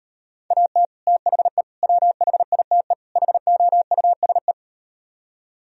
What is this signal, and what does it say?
Morse code, heard as a keyed tone, about 38 wpm, 700 Hz. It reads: AT THE WHITE HOUSE